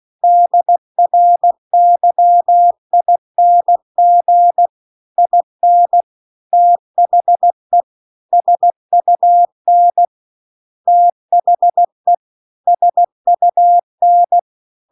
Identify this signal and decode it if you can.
Morse code, heard as a keyed tone, about 16 wpm, 700 Hz. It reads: DRYING IN THE SUN THE SUN
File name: CW-The-Big-Twoheaded-River-2-16WPM